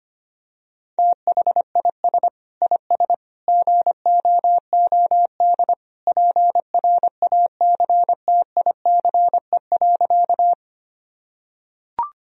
Morse code, heard as a keyed tone, about 25 wpm, 700 Hz. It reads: THIS IS GOOD PRACTICE.
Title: Sentences from Top 1000 Words - Review Part 4 - 25wpm